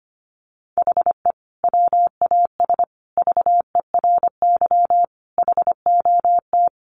Morse code, heard as a keyed tone, about 25 wpm, 700 Hz. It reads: HE WAS VERY HOT
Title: Sentences from Top 1400 Words 25wpm